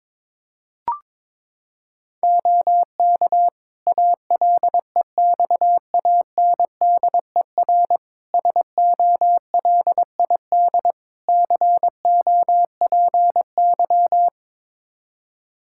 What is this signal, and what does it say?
Morse code, heard as a keyed tone, about 22 wpm, 700 Hz. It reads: OK ALEXANDER SOLID COPY
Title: K1USN SST QSO Practice 22wpm